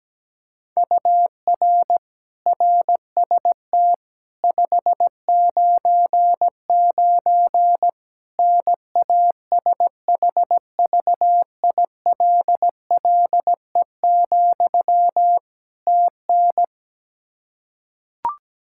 Morse code, heard as a keyed tone, about 17 wpm, 700 Hz. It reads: UR RST 599 NASHVILLE, TN
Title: Practice QSO - Exchange 1 - Beginning - 17wpm